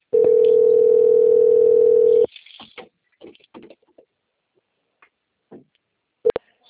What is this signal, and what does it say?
Ringback tone (North America), 440 + 480 Hz.
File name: out- - - - - .72198